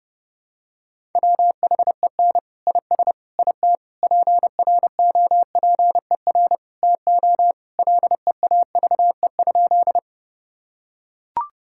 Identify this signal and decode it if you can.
Morse code, heard as a keyed tone, about 32 wpm, 700 Hz. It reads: WHEN IS IT PROPER TO LEAVE?
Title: Sentences from Top 1000 Words - Review Part 5 - 30wpm